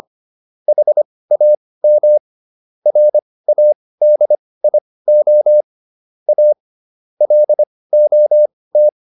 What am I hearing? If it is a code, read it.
Morse code, heard as a keyed tone, about 25 wpm, spaced as for 18 wpm, 600 Hz. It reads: HAM RADIO A LOT